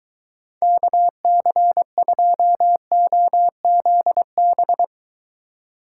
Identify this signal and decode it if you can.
Morse code, heard as a keyed tone, about 20 wpm, 700 Hz. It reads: KC2OZB